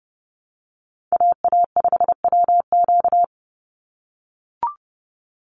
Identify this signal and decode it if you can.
Morse code, heard as a keyed tone, about 30 wpm, 700 Hz. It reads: AA5WQ
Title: USA Call Signs 30wpm